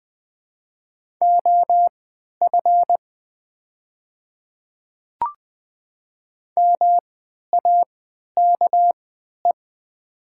Morse code, heard as a keyed tone, about 21 wpm, spaced as for 12 wpm, 700 Hz. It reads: OF MAKE
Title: 3x Character Spacing - Top 500 Words - 20wpm